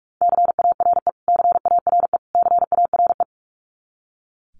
Morse code, heard as a keyed tone, about 44 wpm, 700 Hz. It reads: CARE CARE CARE
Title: Top 300 Words Repeated 45wpm